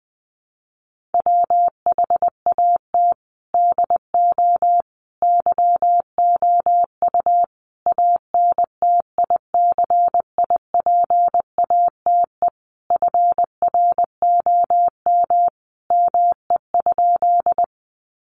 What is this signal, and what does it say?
Morse code, heard as a keyed tone, about 20 wpm, 700 Hz. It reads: WHAT DO YOU ANTICIPATE FROM ME?
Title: Sentences from Top 1500 Words 20wpm